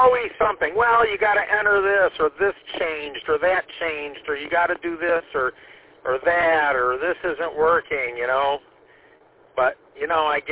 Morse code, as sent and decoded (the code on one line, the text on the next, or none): none